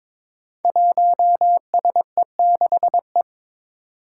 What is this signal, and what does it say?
Morse code, heard as a keyed tone, about 23 wpm, 700 Hz. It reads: 1SE6E
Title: Five Challenging Letter-Number Combinations 22wpm